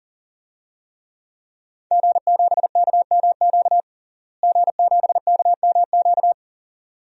Morse code, heard as a keyed tone, about 40 wpm, 700 Hz. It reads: G7KMQ G7KMQ
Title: DXCC-100 Call Signs - 5 Characters - Repeated 2x - 40wpm